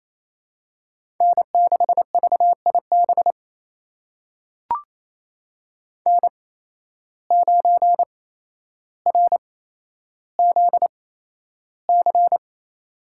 Morse code, heard as a keyed tone, about 28 wpm, spaced as for 12 wpm, 700 Hz. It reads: N6VIB N9RZC
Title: USA Call Signs - 8x Character Spacing - 28wpm